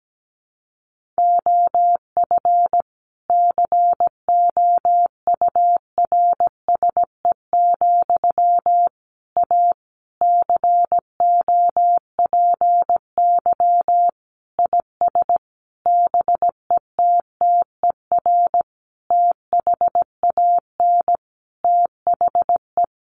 Morse code, heard as a keyed tone, about 17 wpm, 700 Hz. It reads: OF COURSE, A COPY IS BETTER THAN THE
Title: Sentences from Top 1400 Words 17wpm